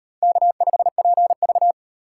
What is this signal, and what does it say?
Morse code, heard as a keyed tone, about 33 wpm, 700 Hz. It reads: K5PV